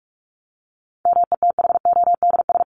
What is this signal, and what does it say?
Morse code, heard as a keyed tone, about 44 wpm, 700 Hz. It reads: METHODS
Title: ES:S Suffix - Top 100 Words with Base Words 45wpm